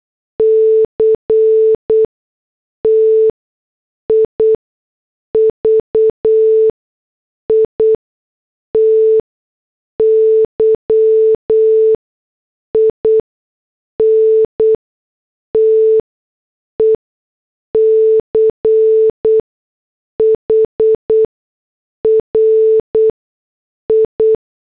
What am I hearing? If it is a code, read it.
Morse code, heard as a keyed tone, about 8 wpm, spaced as for 6 wpm, 440 Hz. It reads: CTIVITY IN TECH RI